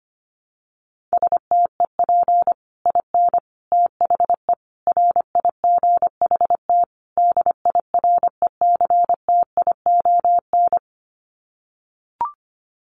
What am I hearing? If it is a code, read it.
Morse code, heard as a keyed tone, about 25 wpm, 700 Hz. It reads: STEP IN THE RIGHT DIRECTION
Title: Sets of 5 Words Encore 25wpm